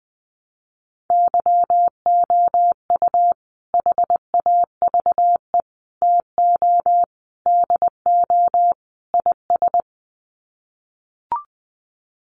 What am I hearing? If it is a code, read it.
Morse code, heard as a keyed tone, about 20 wpm, 700 Hz. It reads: YOU HAVE TO DO IS
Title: Sets of 5 Words 20wpm